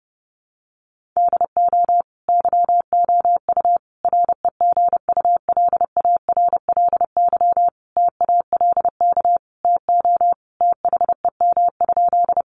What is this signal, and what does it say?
Morse code, heard as a keyed tone, about 30 wpm, 700 Hz. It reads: DO YOU REGULARLY TALK TO THEM?
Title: Sentences from Top 1400 Words 30wpm